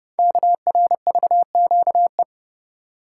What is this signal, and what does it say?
Morse code, heard as a keyed tone, about 30 wpm, 690 Hz. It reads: KRVQE
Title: Code Groups - 5 Letters - 30wpm